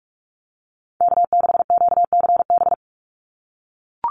Morse code, heard as a keyed tone, about 44 wpm, 700 Hz. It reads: K6QCB
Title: NA SOTA Chaser Callsigns 45wpm